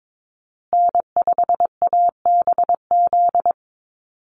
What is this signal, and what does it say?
Morse code, heard as a keyed tone, about 22 wpm, 700 Hz. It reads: N5ABZ
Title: USA Call Signs Repeated 2x 22wpm